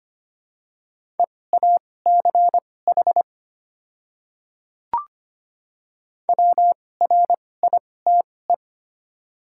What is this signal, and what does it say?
Morse code, heard as a keyed tone, about 25 wpm, spaced as for 18 wpm, 700 Hz. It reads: EACH WRITE
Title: 2x Character Spacing - Top 100 Words - 25wpm